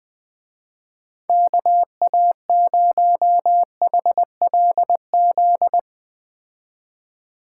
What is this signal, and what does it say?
Morse code, heard as a keyed tone, about 20 wpm, 700 Hz. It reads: KA0HLZ
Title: Speed Racing - USA 2x3 Call Signs 17wpm